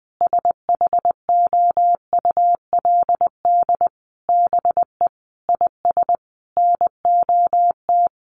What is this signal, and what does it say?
Morse code, heard as a keyed tone, about 20 wpm, 700 Hz. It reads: SHOULD BE IS NOT
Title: Sentences from Top 1300 Words 20wpm